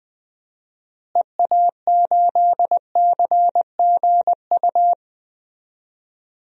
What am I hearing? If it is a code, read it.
Morse code, heard as a keyed tone, about 20 wpm, 700 Hz. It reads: EA8CGU
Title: DXCC-100 Call Signs - 6 Characters - Repeated 2x - 20wpm